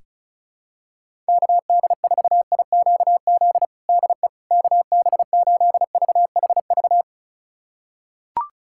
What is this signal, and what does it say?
Morse code, heard as a keyed tone, about 36 wpm, 700 Hz. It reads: KD4IQZ DE KB8VHV
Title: USA Callsign DE Callsign - 6 Characters - 35wpm